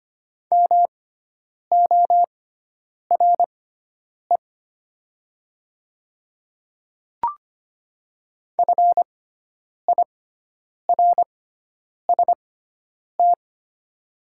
Morse code, heard as a keyed tone, about 25 wpm, spaced as for 9 wpm, 700 Hz. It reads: MORE FIRST
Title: 6x Character Spacing - Top 100 Words - 25wpm